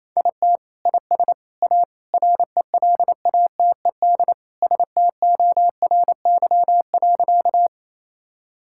Morse code, heard as a keyed tone, about 27 wpm, 700 Hz. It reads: IT IS A RELATED STORY.
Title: Sentences from Top 1000 Words - Review Part 5 - 28wpm